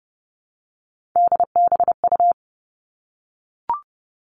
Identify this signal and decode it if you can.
Morse code, heard as a keyed tone, about 30 wpm, 700 Hz. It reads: DBU
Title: All Cut Numbers - 3-Digit 30wpm